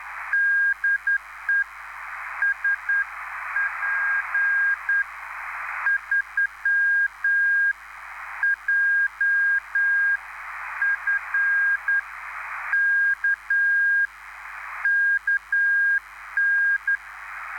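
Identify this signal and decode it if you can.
Morse code, heard as a keyed tone, about 10 wpm, spaced as for 8 wpm, 1600 Hz, with radio noise. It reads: DE SP3JFK KN
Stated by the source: Live CW QSO between PA2BX and SP3JFK on the 14055 kHz QRS frequency, at a basic 8 wpm.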